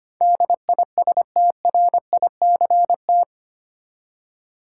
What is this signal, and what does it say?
Morse code, heard as a keyed tone, about 25 wpm, 700 Hz. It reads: DISTRICT
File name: 8 Letter Words - First 3 Letters Given - 25wpm